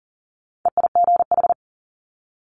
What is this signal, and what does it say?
Morse code, heard as a keyed tone, about 40 wpm, 700 Hz. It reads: EIGH
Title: Multi-letter Phonemes Quick Fire 40wpm